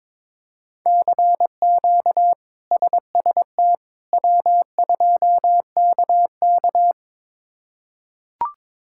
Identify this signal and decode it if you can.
Morse code, heard as a keyed tone, about 22 wpm, 700 Hz. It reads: CQ SST W2KK
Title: K1USN SST QSO Practice - 22wpm